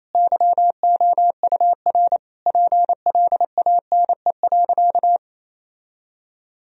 Morse code, heard as a keyed tone, about 27 wpm, 700 Hz. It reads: YOUR PLANE.
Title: Sentences from Top 1000 Words - Review Part 3 - 28wpm